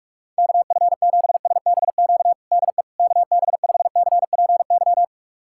Morse code, heard as a keyed tone, about 46 wpm, 700 Hz. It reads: KF7SBQ DE KB5CPY
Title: USA Callsign DE Callsign 45wpm